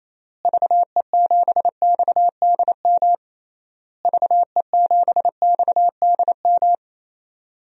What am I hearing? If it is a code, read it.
Morse code, heard as a keyed tone, about 28 wpm, 700 Hz. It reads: VE7XDM VE7XDM
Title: DXCC-100 Call Signs - 6 Characters - Repeated 3x - 28wpm